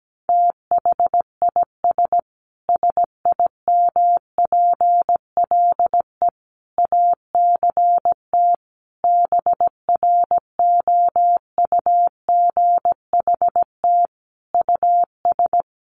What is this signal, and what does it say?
Morse code, heard as a keyed tone, about 17 wpm, 700 Hz. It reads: THIS SIMPLE ACT BROUGHT US
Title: Sentences from Top 1500 Words 17wpm